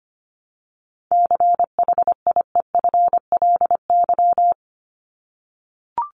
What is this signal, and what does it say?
Morse code, heard as a keyed tone, about 24 wpm, 700 Hz. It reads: CHIEFLY
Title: LY Suffix - Top 100 Words with Base Words 25wpm